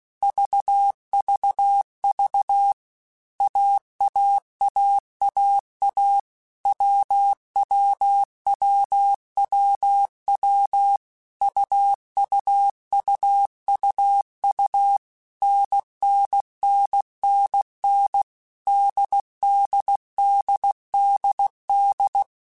Morse code, heard as a keyed tone, about 16 wpm, 800 Hz. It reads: VVV AAAAA WWWWW UUUUU NNNNN DDDDD
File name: Basic Letter Groups By7Groups / AWUNDG / AWUNDG@15